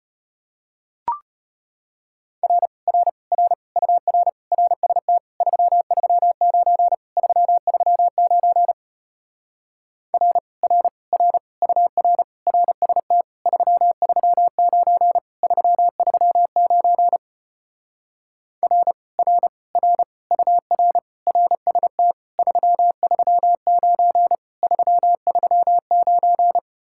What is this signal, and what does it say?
Morse code, heard as a keyed tone, about 33 wpm, 700 Hz. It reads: R R R UR RST 339 339 R R R UR RST 339 339 R R R UR RST 339 339